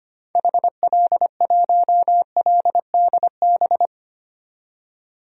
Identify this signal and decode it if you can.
Morse code, heard as a keyed tone, about 25 wpm, 700 Hz. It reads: HL1LDB
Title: DXCC-100 Call Signs - 6 Characters - 25wpm